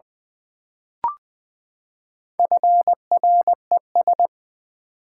Morse code, heard as a keyed tone, about 20 wpm, 700 Hz. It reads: FRES
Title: Build A Word - 4 Letters At A Time - 8-Letter Words - 20wpm